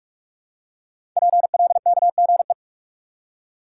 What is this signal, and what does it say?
Morse code, heard as a keyed tone, about 44 wpm, 700 Hz. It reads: PLKGE
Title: Code Groups - 5 Letters - 45wpm